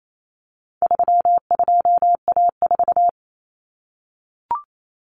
Morse code, heard as a keyed tone, about 28 wpm, 700 Hz. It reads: 32A4
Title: Common Cut Numbers - 4-Digit 28wpm